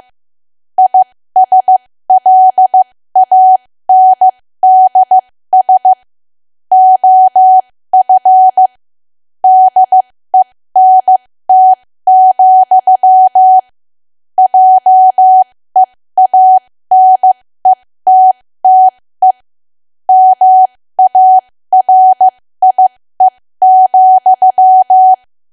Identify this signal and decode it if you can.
Morse code, heard as a keyed tone, about 15 wpm, spaced as for 13 wpm, 750 Hz. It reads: ISLANDS OF DENT, JEANETTE MARIE,
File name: cwa / W1AW-Level2 / 13WPM